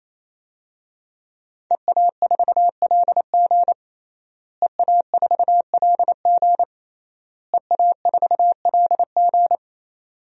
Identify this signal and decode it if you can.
Morse code, heard as a keyed tone, about 28 wpm, 700 Hz. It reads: EA4LG EA4LG EA4LG
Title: DXCC-100 Call Signs - 5 Characters - Repeated 3x - 28wpm